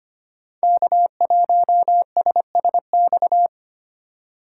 Morse code, heard as a keyed tone, about 25 wpm, 700 Hz. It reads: K1SSX
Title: Code Groups - 5 Letter:Numbers - 25wpm